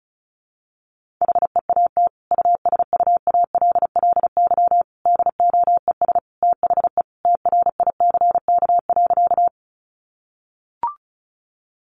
Morse code, heard as a keyed tone, about 34 wpm, 700 Hz. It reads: HEAT USUALLY DOES THE TRICK.
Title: Sentences from Top 1500 Words 35wpm